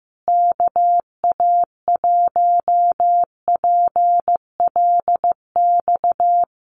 Morse code, heard as a keyed tone, about 15 wpm, 700 Hz. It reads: KA1PLX
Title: USA Call Signs Repeated 3x 15wpm